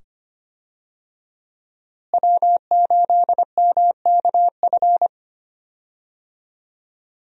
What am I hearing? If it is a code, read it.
Morse code, heard as a keyed tone, about 25 wpm, 700 Hz. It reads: W8MKF
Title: USA Call Signs - 5 Characters - Repeated 2x - 25wpm